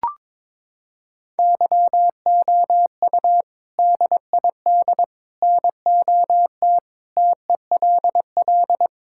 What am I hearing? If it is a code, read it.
Morse code, heard as a keyed tone, about 22 wpm, 700 Hz. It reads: YOU DID NOT TELL
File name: practice / 22wpm / Sentences with 3-Word Phrases - 22wpm